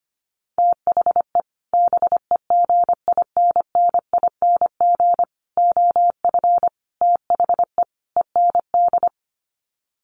Morse code, heard as a keyed tone, about 25 wpm, 700 Hz. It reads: THE BEGINNING OF THE END
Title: Sets of 5 Words Encore2 25wpm